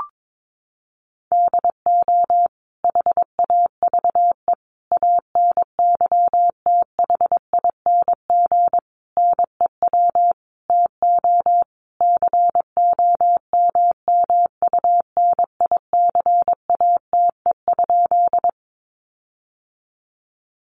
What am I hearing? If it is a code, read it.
Morse code, heard as a keyed tone, about 22 wpm, 700 Hz. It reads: DO HAVE ANYTHING NEW TO COMMUNICATE?
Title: Sentences from Top 1300 Words 22wpm